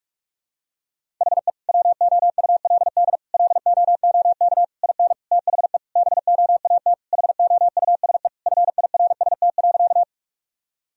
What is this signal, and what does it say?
Morse code, heard as a keyed tone, about 41 wpm, 700 Hz. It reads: HE WOULD LOOK IN THE BOAT HOUSE FIRST.
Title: Sentences from Top 1000 Words - Review Part 3 - 45wpm